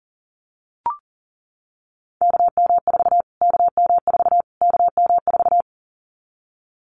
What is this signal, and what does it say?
Morse code, heard as a keyed tone, about 40 wpm, 700 Hz. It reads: KM4 KM4 KM4
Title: Callsign Prefixes for Top 25 DXCC - Repeated 3x - 40wpm